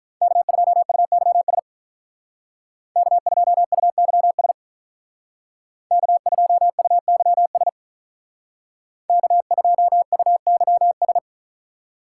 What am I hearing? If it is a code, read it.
Morse code, heard as a keyed tone, about 43 wpm, 700 Hz. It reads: K2UYS K2UYS K2UYS K2UYS
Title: Speed Racing - USA 1x3 Call Signs 35wpm